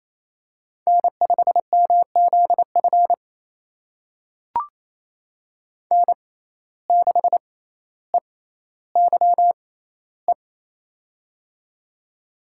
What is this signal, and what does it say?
Morse code, heard as a keyed tone, about 29 wpm, spaced as for 15 wpm, 700 Hz. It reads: N5MZF N6EYE
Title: USA Call Signs - 5 Characters - 6x Character Spacing - 28wpm